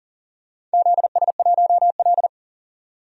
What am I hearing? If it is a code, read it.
Morse code, heard as a keyed tone, about 38 wpm, 700 Hz. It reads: ZS1L
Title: DXCC-100 Call Signs - 4 Characters - 40wpm